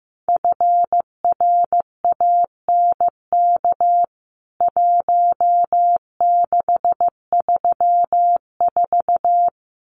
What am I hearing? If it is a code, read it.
Morse code, heard as a keyed tone, about 15 wpm, 700 Hz. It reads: FRANK 1634
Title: CWT Exchange 15wpm